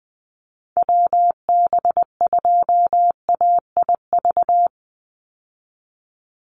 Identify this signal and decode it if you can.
Morse code, heard as a keyed tone, about 20 wpm, 700 Hz. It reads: WB2AIV
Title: Call Signs Repeated 3x - 20wpm